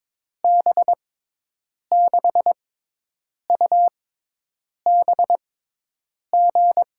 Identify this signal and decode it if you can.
Morse code, heard as a keyed tone, about 22 wpm, spaced as for 8 wpm, 700 Hz. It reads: B6UBG
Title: USA Call Signs - 6 Characters - 6x Character Spacing - 22wpm